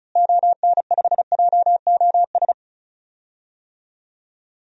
Morse code, heard as a keyed tone, about 34 wpm, 700 Hz. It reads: ON5JOS
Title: DXCC-100 Call Signs - 6 Characters - Repeated 2x - 35wpm